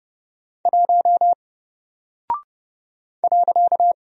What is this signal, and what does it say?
Morse code, heard as a keyed tone, about 30 wpm, 700 Hz. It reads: 1 .
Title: Single Character (Letter:Number:Punctuation:Prosign) Alt 30wpm